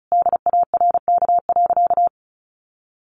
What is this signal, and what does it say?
Morse code, heard as a keyed tone, about 34 wpm, 700 Hz. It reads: DARK.
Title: Sentences from Top 1400 Words 35wpm